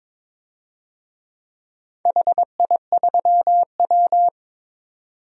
Morse code, heard as a keyed tone, about 22 wpm, 700 Hz. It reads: HI3W
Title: DXCC-100 Call Signs - 4 Characters - 22wpm